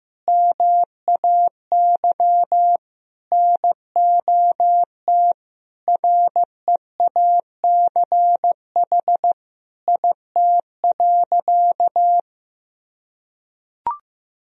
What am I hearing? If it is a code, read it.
Morse code, heard as a keyed tone, about 15 wpm, 700 Hz. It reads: MAY NOT REACH IT.
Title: Sentences from Top 1000 Words - Review Part 2 - 15wpm